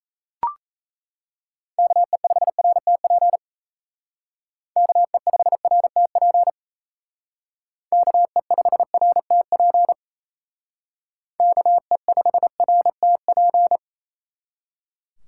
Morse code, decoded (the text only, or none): KE5RTP KE5RTP KE5RTP KE5RTP